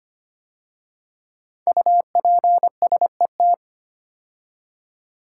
Morse code, heard as a keyed tone, about 25 wpm, 700 Hz. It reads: UPSET
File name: 5 Letter Words - First 3 Letters Given - 25wpm